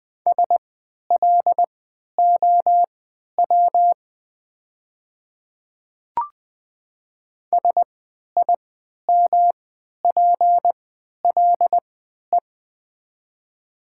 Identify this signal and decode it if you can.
Morse code, heard as a keyed tone, about 20 wpm, spaced as for 11 wpm, 700 Hz. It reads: SLOW SIMPLE
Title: 3x Character Spacing - Top 500 Words - 20wpm